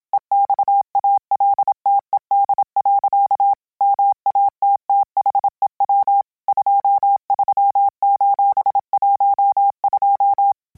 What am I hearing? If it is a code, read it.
Morse code, heard as a keyed tone, about 27 wpm, 800 Hz. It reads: EXALTED. MATTHEW 23:12